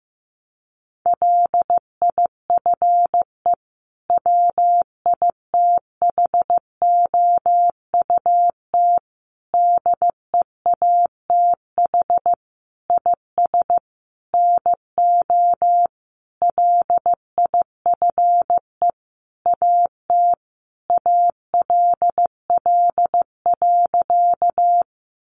Morse code, heard as a keyed tone, about 15 wpm, 700 Hz. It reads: LIFE WITHOUT DEATH IS NO LIFE AT ALL.
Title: Sentences from Top 1100 Words 15wpm